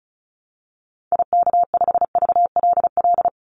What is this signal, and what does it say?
Morse code, heard as a keyed tone, about 34 wpm, 700 Hz. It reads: IK5VLL
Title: Call Signs - 35wpm